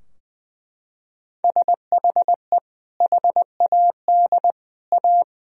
Morse code, heard as a keyed tone, about 21 wpm, 700 Hz. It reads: SHE HAD A
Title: Sentences from Top 1000 Words - Review Part 4 - 20wpm